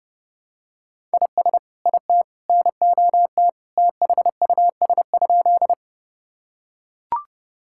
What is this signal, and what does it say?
Morse code, heard as a keyed tone, about 30 wpm, 700 Hz. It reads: IS IT NOT THUS?
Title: Sentences from Top 1000 Words - Review Part 5 - 30wpm